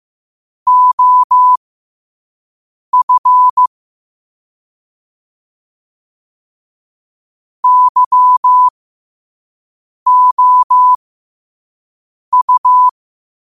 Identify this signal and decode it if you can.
Morse code, heard as a keyed tone, about 15 wpm, spaced as for 5 wpm, 1000 Hz. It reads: OF YOU
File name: qotdfiles / 5wpm